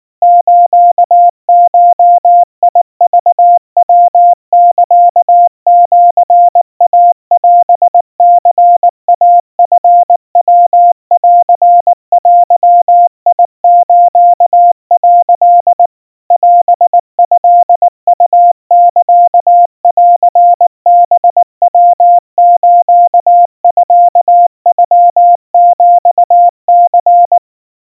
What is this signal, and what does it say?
Morse code (Japanese Wabun code), heard as a keyed tone, about 19 wpm, 700 Hz. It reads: スゴクヤサシイオニイチヤンデス」 オトウサンハヤスミノヒニ
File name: kawa / Morse sample / kamyu 70 cpm 700 Hz Crispy